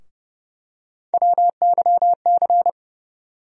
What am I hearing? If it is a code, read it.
Morse code, heard as a keyed tone, about 30 wpm, 700 Hz. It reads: WYC
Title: USA Call Sign Postfixes - 3 Letters 30wpm